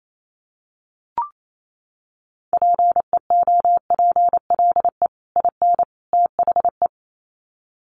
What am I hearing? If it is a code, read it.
Morse code, heard as a keyed tone, about 28 wpm, 700 Hz. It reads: PEOPLE IN THE
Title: Sets of 3 Words 28wpm